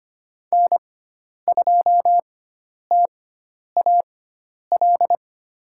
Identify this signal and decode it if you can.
Morse code, heard as a keyed tone, about 25 wpm, spaced as for 10 wpm, 700 Hz. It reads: N2TAL
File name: Lesson 16g - Letter c - Call Signs 25wpm